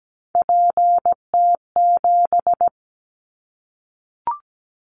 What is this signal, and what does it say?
Morse code, heard as a keyed tone, about 17 wpm, 700 Hz. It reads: PT7
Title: Callsign Prefixes for Top 50 DXCC - 17wpm